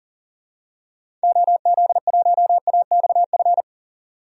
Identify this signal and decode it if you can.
Morse code, heard as a keyed tone, about 39 wpm, 700 Hz. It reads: OZ1AXF
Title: DXCC-100 Call Signs - 6 Characters - 40wpm